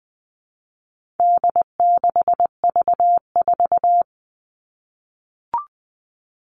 Morse code, heard as a keyed tone, about 20 wpm, 700 Hz. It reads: D6V4
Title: All Cut Numbers - 4-Digit 20wpm